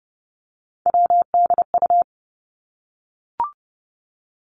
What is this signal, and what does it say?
Morse code, heard as a keyed tone, about 30 wpm, 700 Hz. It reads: WDU
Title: Three Challenging Letter-Number Combinations 30wpm